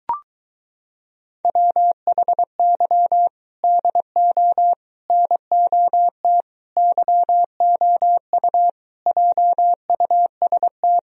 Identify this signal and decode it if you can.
Morse code, heard as a keyed tone, about 23 wpm, 700 Hz. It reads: WHY DO NOT YOU JUST